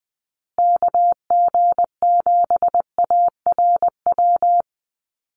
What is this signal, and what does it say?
Morse code, heard as a keyed tone, about 20 wpm, 700 Hz. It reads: KG7ARW
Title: USA Call Signs Repeated 3x 20wpm